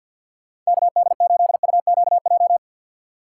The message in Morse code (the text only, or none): KD8UQJ